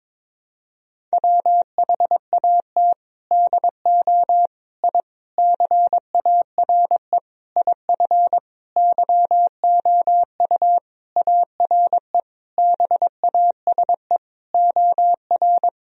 Morse code, heard as a keyed tone, about 22 wpm, 700 Hz. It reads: WHAT DO I CARE IF YOU ARE BASE OR
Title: Sentences from Top 1000 Words - Review Part 1 - 22wpm